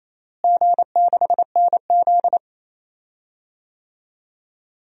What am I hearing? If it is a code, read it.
Morse code, heard as a keyed tone, about 27 wpm, 700 Hz. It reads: G6NZ
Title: DXCC-100 Call Signs - 4 Characters - Repeated 2x - 28wpm